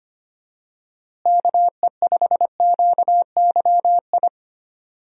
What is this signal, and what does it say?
Morse code, heard as a keyed tone, about 25 wpm, 700 Hz. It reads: KE5QYI
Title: USA Call Signs - 6 Characters - Repeated 2x - 25wpm